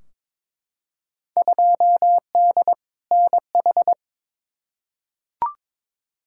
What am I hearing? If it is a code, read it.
Morse code, heard as a keyed tone, about 22 wpm, 700 Hz. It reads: 2D NH
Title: Field Day Exchanges Encore - 22wpm